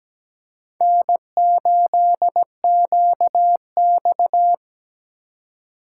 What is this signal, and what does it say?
Morse code, heard as a keyed tone, about 17 wpm, 700 Hz. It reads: N8QX
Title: USA Call Signs - 4 Characters - 2x Character Spacing - 17wpm